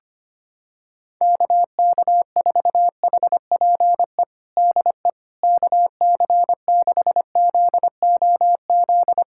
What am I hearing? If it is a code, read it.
Morse code, heard as a keyed tone, about 25 wpm, 700 Hz. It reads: KK4HPE DE KC6ZOZ
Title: USA Callsign DE Callsign - 6 Characters - 25wpm